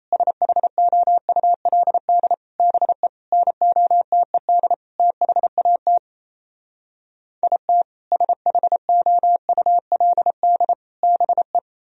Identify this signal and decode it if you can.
Morse code, heard as a keyed tone, about 38 wpm, 700 Hz. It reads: SHOULD BE NOTED THAT IT SHOULD BE